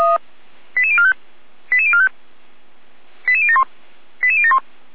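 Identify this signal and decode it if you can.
Five-tone selective call (ZVEI), heard as quick runs of five tones, tones of about 70 ms. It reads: 88045 88045 88071 88071
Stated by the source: ZVEI five-tone call 88163 received by SDR.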